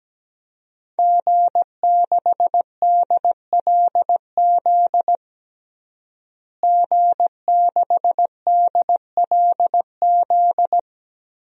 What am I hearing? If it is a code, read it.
Morse code, heard as a keyed tone, about 17 wpm, 700 Hz. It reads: G6DLZ G6DLZ
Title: DXCC-100 Call Signs - 5 Characters - Repeated 3x - 17wpm